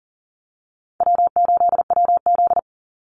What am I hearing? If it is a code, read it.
Morse code, heard as a keyed tone, about 40 wpm, 700 Hz. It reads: W8WZ
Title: Call Signs - 40wpm